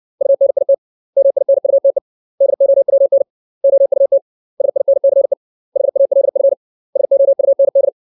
Morse code, heard as a keyed tone, about 60 wpm, 550 Hz. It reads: UNIT MINUTE DOWN OUT SINCE HALF SOUND
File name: 632 words / 632 Words 60WPM